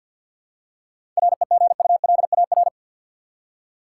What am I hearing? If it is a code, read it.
Morse code, heard as a keyed tone, about 52 wpm, 700 Hz. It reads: REGULAR
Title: Seven Letter Words - 50wpm